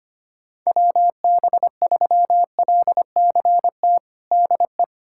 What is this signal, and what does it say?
Morse code, heard as a keyed tone, about 25 wpm, 700 Hz. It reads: WB3LCT DE
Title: USA Callsign DE Callsign 25wpm